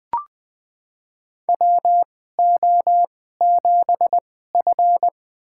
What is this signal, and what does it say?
Morse code, heard as a keyed tone, about 20 wpm, spaced as for 15 wpm, 700 Hz. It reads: WO7F
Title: USA Call Signs - 4 Characters - 2x Character Spacing - 20wpm